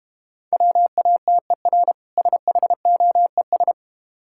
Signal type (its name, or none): Morse code